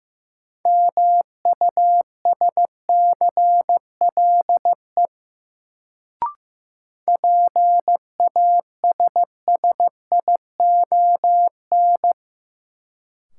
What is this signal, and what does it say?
Morse code, heard as a keyed tone, about 15 wpm, 700 Hz. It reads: MUSCLE PASSION
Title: Sentences from Top 1300 Words - New Word Drill - 15wpm